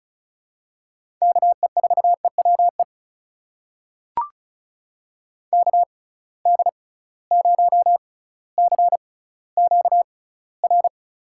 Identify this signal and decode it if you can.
Morse code, heard as a keyed tone, about 36 wpm, spaced as for 18 wpm, 690 Hz. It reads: KE4EWE KD0CQR
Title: USA Call Signs - 6 Characters - 6x Character Spacing - 35wpm